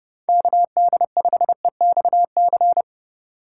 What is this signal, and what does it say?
Morse code, heard as a keyed tone, about 30 wpm, 700 Hz. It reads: KD5EXC
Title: USA Call Signs - 2x Character Spacing - 30wpm